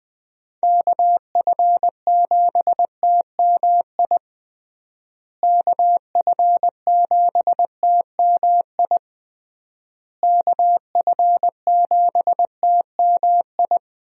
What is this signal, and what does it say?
Morse code, heard as a keyed tone, about 20 wpm, 700 Hz. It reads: KF7TMI KF7TMI KF7TMI
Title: USA Call Signs - 6 Characters - Repeated 3x - 20wpm